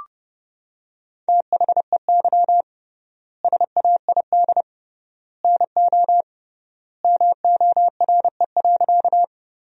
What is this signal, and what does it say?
Morse code, heard as a keyed tone, about 30 wpm, 700 Hz. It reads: THEY SAID NO MORE.
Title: 3x Word-Spacing - Sentences from Top 100 Words - 30wpm